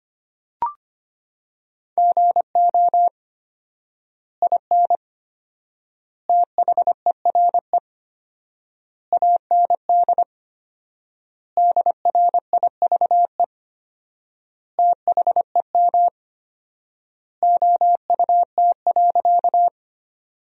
Morse code, heard as a keyed tone, about 25 wpm, 700 Hz. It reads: GO IN THERE AND DRIVE THEM OUT.
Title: 4x Word-Spacing - Sentences from Top 500 Words - 25wpm